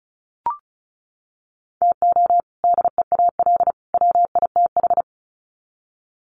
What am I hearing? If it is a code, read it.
Morse code, heard as a keyed tone, about 34 wpm, 700 Hz. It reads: TO DEAL WITH
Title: Sets of 3 Words 35wpm